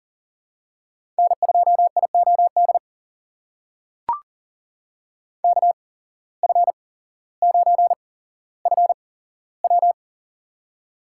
Morse code, heard as a keyed tone, about 40 wpm, spaced as for 17 wpm, 700 Hz. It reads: N2IOD KF9FW
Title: USA Call Signs - 5 Characters - 8x Character Spacing - 40wpm